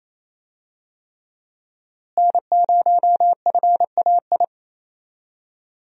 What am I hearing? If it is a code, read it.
Morse code, heard as a keyed tone, about 28 wpm, 700 Hz. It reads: N0FAI
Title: USA Call Signs - 5 Characters - Repeated 3x - 28wpm